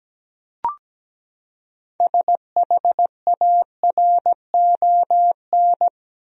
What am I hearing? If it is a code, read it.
Morse code, heard as a keyed tone, about 17 wpm, 700 Hz. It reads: SHARON
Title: USA Common Names - 17wpm